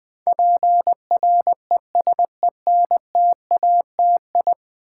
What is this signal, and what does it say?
Morse code, heard as a keyed tone, about 20 wpm, 700 Hz. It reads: PRESENTATI
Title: Build A Word - 5 Letters At A Time - 12-Letter Words - 20wpm